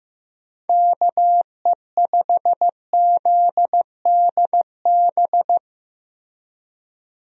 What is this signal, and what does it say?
Morse code, heard as a keyed tone, about 16 wpm, 700 Hz. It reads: KE5ZDB
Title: USA Call Signs - 6 Characters - Repeated 2x - 15wpm